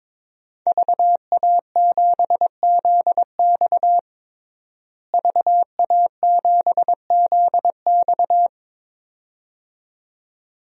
Morse code, heard as a keyed tone, about 22 wpm, 700 Hz. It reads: VA7ZX VA7ZX
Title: DXCC-100 Call Signs - 5 Characters - Repeated 2x - 22wpm